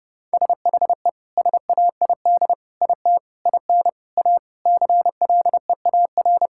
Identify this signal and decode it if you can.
Morse code, heard as a keyed tone, about 32 wpm, 700 Hz. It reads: SHE SAID IT IN A CLEAR
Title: Sentences from Top 1600 Words 30wpm